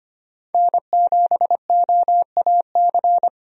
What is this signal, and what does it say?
Morse code, heard as a keyed tone, about 25 wpm, 700 Hz. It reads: N7OAC
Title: USA Call Signs - 5 Characters - 25wpm